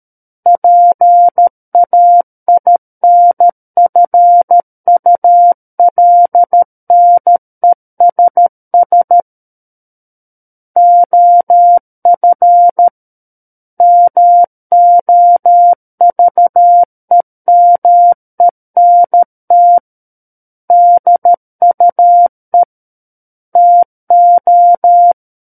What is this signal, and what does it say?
Morse code, heard as a keyed tone, about 13 wpm, 700 Hz. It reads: PAINFULNESS OF MOVEMENT DUE TO
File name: Book One / 04-Chapter 4 - 13 WPM